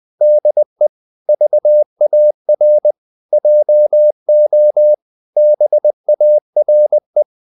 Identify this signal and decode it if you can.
Morse code, heard as a keyed tone, about 20 wpm, 600 Hz. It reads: DE VAR JO BARE
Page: Ferie 20 wpm